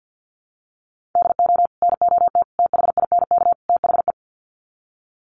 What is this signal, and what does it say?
Morse code, heard as a keyed tone, about 48 wpm, 700 Hz. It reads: DO NOT THINK THE
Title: Sets of 4 Words 50wpm